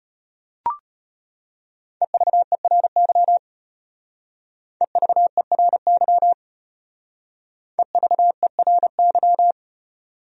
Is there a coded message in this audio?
Yes: Morse code from 2.0 to 9.5 s, heard as a keyed tone, about 32 wpm, 700 Hz. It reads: EVERY EVERY EVERY